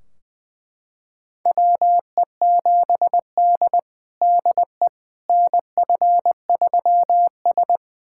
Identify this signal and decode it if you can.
Morse code, heard as a keyed tone, about 21 wpm, 700 Hz. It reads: WE7D DE NF3S
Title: USA Callsign DE Callsign - 4 Characters - 20wpm